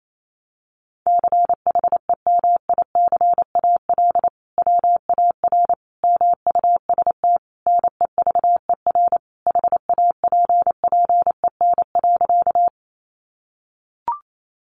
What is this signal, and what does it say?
Morse code, heard as a keyed tone, about 28 wpm, 700 Hz. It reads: CHEMICAL WAR MUST NEVER HAPPEN.
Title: Sentences from Top 1200 Words 28wpm